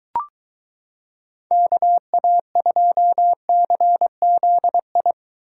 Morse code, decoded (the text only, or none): KA2CZI